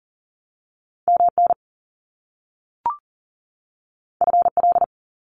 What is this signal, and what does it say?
Morse code, heard as a keyed tone, about 42 wpm, 700 Hz. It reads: MN FL
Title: US States 40wpm